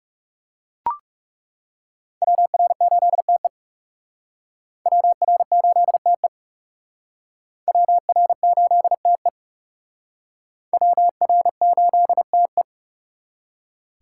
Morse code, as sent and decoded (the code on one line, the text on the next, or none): .-- .-. ---.. - . / .-- .-. ---.. - . / .-- .-. ---.. - . / .-- .-. ---.. - .
WR8TE WR8TE WR8TE WR8TE